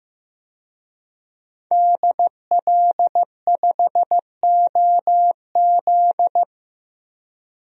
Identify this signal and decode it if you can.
Morse code, heard as a keyed tone, about 15 wpm, 700 Hz. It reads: DL5OZ
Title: DXCC-100 Call Signs - 5 Characters - 15wpm